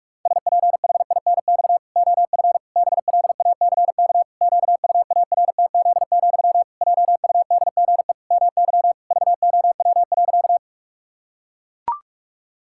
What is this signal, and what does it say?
Morse code, heard as a keyed tone, about 41 wpm, 700 Hz. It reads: SPHINX OF BLACK QUARTZ, JUDGE MY VOW.
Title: Pangram Challenge v2 - 45wpm